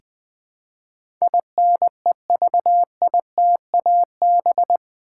Morse code, heard as a keyed tone, about 20 wpm, 700 Hz. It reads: INEVITAB
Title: Build A Word - 4 Letters At A Time - 10-Letter Words - 20wpm